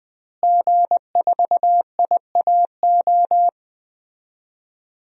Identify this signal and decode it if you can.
Morse code, heard as a keyed tone, about 20 wpm, 700 Hz. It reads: G4IAO
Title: DXCC-100 Call Signs - 5 Characters - Repeated 2x - 20wpm